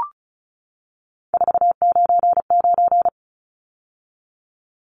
Morse code, heard as a keyed tone, about 34 wpm, 700 Hz. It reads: 499
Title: Numbers - 3-Digit 35wpm